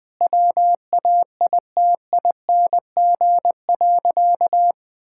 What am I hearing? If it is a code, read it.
Morse code, heard as a keyed tone, about 20 wpm, 700 Hz. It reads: WAITING.
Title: Sentences with 5-Word Phrases - 20wpm